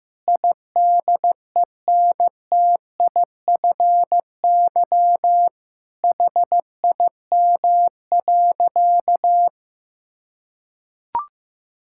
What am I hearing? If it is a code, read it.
Morse code, heard as a keyed tone, about 15 wpm, 700 Hz. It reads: IDENTIFY HIM.
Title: Sentences from Top 1000 Words - Review Part 6 - 15wpm